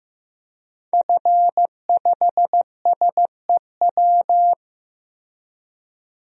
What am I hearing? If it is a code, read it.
Morse code, heard as a keyed tone, about 15 wpm, 700 Hz. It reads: F5SEW
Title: DXCC-100 Call Signs - 5 Characters - Repeated 2x - 15wpm